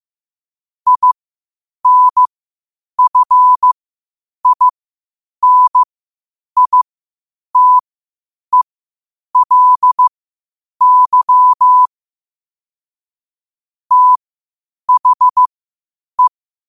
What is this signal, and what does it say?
Morse code, heard as a keyed tone, about 15 wpm, spaced as for 8 wpm, 1000 Hz. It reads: INFINITELY THE
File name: qotdfiles / 8wpm